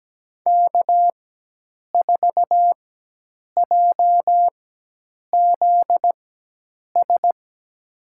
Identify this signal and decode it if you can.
Morse code, heard as a keyed tone, about 17 wpm, spaced as for 8 wpm, 700 Hz. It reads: K4JZS
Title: USA Call Signs - 5 Characters - 4x Character Spacing - 17wpm